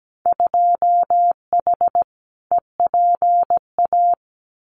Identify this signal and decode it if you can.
Morse code, heard as a keyed tone, about 17 wpm, 700 Hz. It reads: 2H EPA
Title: Winter Field Day Exchanges 17wpm